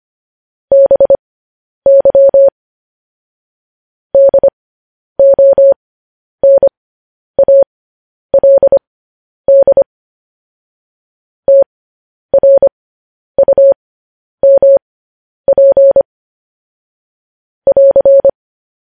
Morse code, heard as a keyed tone, about 25 wpm, spaced as for 10 wpm, 550 Hz. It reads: BY DONALD TRUMP <AR>
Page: news headlines in morse code at 10 wpm